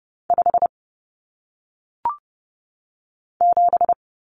Morse code, heard as a keyed tone, about 30 wpm, 700 Hz. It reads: 5 7
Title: Common Cut Numbers - 1-Digit 30wpm